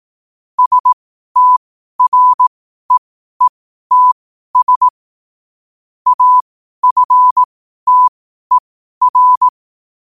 Morse code, heard as a keyed tone, about 18 wpm, spaced as for 12 wpm, 1000 Hz. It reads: STREETS AFTER
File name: 12wpm